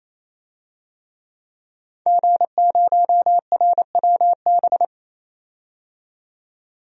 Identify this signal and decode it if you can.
Morse code, heard as a keyed tone, about 28 wpm, 700 Hz. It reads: G0RWB
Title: DXCC-100 Call Signs - 5 Characters - 28wpm